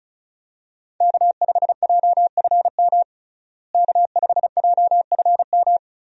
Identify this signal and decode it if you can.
Morse code, heard as a keyed tone, about 34 wpm, 700 Hz. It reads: K5JFM K5JFM
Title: USA Call Signs - 5 Characters - Repeated 3x - 35wpm